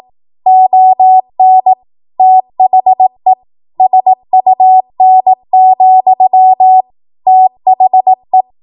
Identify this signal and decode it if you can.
Morse code, heard as a keyed tone, about 18 wpm, 750 Hz. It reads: ON THE SUN, THE
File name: Morse / 18WPM